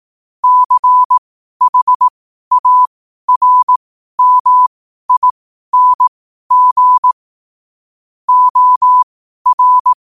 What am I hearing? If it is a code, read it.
Morse code, heard as a keyed tone, about 18 wpm, spaced as for 12 wpm, 1000 Hz. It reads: CHARMING OR
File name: qotdfiles / 12wpm